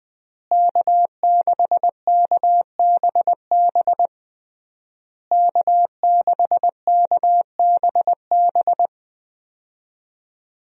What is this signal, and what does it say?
Morse code, heard as a keyed tone, about 21 wpm, 700 Hz. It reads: K6KBB K6KBB
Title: USA Call Signs - 5 Characters - Repeated 3x - 20wpm